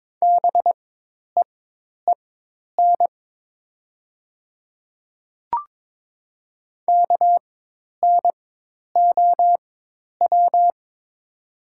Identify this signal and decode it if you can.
Morse code, heard as a keyed tone, about 22 wpm, spaced as for 10 wpm, 700 Hz. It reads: BEEN KNOW
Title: 4x Character Spacing - Top 100 Words - 22wpm